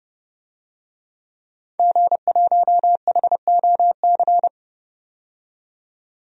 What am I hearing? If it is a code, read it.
Morse code, heard as a keyed tone, about 32 wpm, 700 Hz. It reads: G1HOC
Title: DXCC-100 Call Signs - 5 Characters - 30wpm